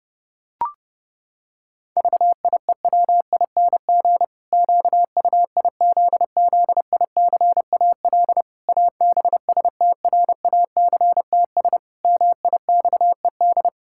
Morse code, heard as a keyed tone, about 28 wpm, 700 Hz. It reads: VIEWING QUIZZICAL ABSTRACTS MIXED
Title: Pangram Challenge v2 - 30wpm